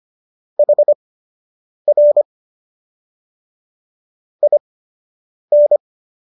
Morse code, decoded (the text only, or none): HR IN